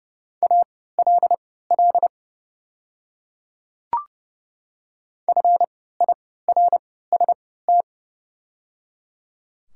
Morse code, heard as a keyed tone, about 30 wpm, spaced as for 17 wpm, 700 Hz. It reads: ALL FIRST
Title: 3x Character Spacing - Top 100 Words - 30wpm